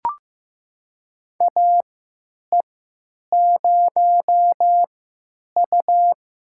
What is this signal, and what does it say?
Morse code, heard as a keyed tone, about 15 wpm, spaced as for 8 wpm, 700 Hz. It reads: AE0U
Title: USA Call Signs - 4 Characters - 3x Character Spacing - 15wpm